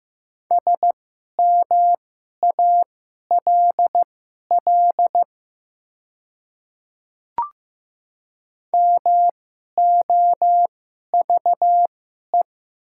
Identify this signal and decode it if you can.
Morse code, heard as a keyed tone, about 15 wpm, spaced as for 11 wpm, 700 Hz. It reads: SMALL MOVE
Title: 2x Character Spacing - Top 300 Words - 15wpm